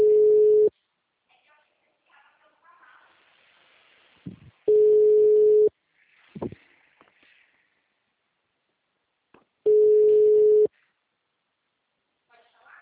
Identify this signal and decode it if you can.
Ringback tone (Europe, 425 Hz), with speech.
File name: out- -2021- - - .31695